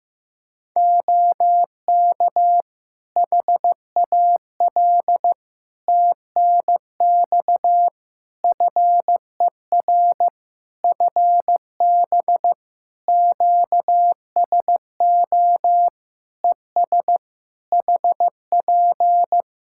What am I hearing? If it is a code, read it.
Morse code, heard as a keyed tone, about 15 wpm, 700 Hz. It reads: OK HAL TNX FER FB QSO ES HP
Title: Practice QSO - Exchange 4 - Ending - 15wpm